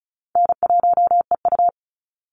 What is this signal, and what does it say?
Morse code, heard as a keyed tone, about 36 wpm, 700 Hz. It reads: N1EU
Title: NA SOTA Activator Callsigns 35wpm